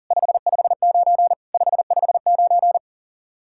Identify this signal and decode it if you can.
Morse code, heard as a keyed tone, about 48 wpm, 700 Hz. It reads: 559 559
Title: Speed-Racing Common QSO Phrases Encore - 40wpm